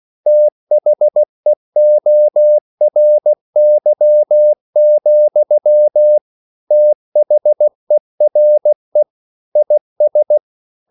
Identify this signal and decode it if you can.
Morse code, heard as a keyed tone, about 16 wpm, 600 Hz. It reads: THEORY, THERE IS
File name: CW / Practice / EnglishQuotations0116